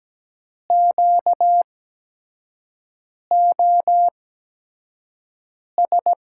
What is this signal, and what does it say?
Morse code, heard as a keyed tone, about 17 wpm, spaced as for 5 wpm, 700 Hz. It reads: QOS
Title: USA Call Signs - 8x Character Spacing - 17wpm